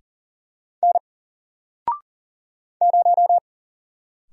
Morse code, heard as a keyed tone, about 40 wpm, 700 Hz. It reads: N 0
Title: Single Letter-Number 40wpm